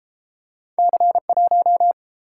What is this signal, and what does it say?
Morse code, heard as a keyed tone, about 31 wpm, 700 Hz. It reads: C1